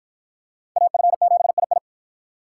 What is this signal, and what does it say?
Morse code, heard as a keyed tone, about 53 wpm, 700 Hz. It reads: AF7II